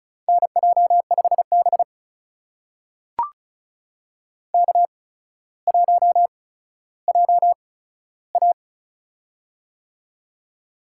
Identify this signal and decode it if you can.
Morse code, heard as a keyed tone, about 36 wpm, spaced as for 15 wpm, 700 Hz. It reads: NJ5B K1JA
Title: USA Call Signs - 4 Characters - 8x Character Spacing - 35wpm